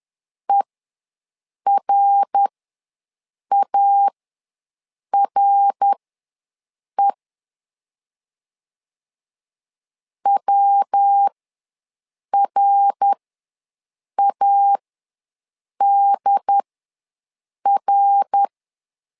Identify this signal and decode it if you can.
Morse code, heard as a keyed tone, about 11 wpm, spaced as for 6 wpm, 800 Hz. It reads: ERARE WRADR
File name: LEZIONI / 3 MWRO / 3 MWR ws 10wpm 2151